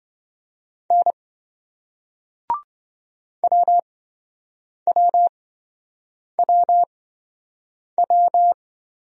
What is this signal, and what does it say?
Morse code, heard as a keyed tone, about 27 wpm, spaced as for 7 wpm, 700 Hz. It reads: N WWWW